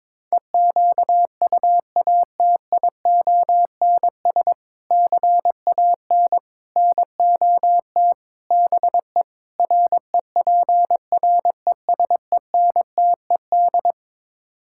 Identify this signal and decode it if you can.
Morse code, heard as a keyed tone, about 22 wpm, 700 Hz. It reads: EQUATIONS CAN NOT BE REPRESENTED
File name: practice / 22wpm / Build A Phrase - 5-Word Phrases - 22wpm